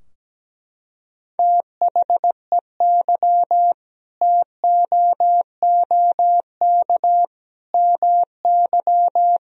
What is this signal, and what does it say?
Morse code, heard as a keyed tone, about 17 wpm, 700 Hz. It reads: THEY TOOK MY
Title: Sentences from Top 1000 Words - Review Part 2 - 17wpm